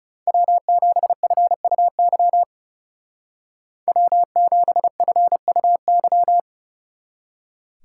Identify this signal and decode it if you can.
Morse code, heard as a keyed tone, about 34 wpm, 700 Hz. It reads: W7FUY W7FUY
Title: Speed Racing - USA 1x3 Call Signs 30wpm